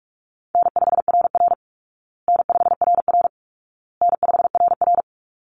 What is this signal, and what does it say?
Morse code, heard as a keyed tone, about 44 wpm, 700 Hz. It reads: N5RR N5RR N5RR
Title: Easy Call Signs Repeated 3x - 45wpm